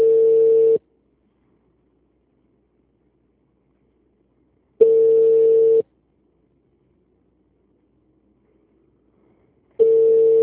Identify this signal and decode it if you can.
Ringback tone (China, 450 Hz).
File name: out- -2006- - - .196